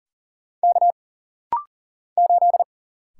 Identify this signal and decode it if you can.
Morse code, heard as a keyed tone, about 40 wpm, 700 Hz. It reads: K 8
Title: Single Letter-Number - Rapid-Fire 40wpm